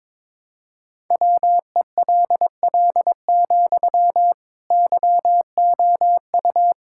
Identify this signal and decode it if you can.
Morse code, heard as a keyed tone, about 22 wpm, 700 Hz. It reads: WELL, YOU
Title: Sentences from Top 1000 Words - Review Part 6 - 22wpm